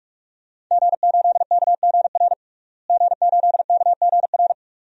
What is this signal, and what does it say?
Morse code, heard as a keyed tone, about 46 wpm, 700 Hz. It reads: G8KGR G8KGR
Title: DXCC-100 Call Signs - 5 Characters - Repeated 2x - 45wpm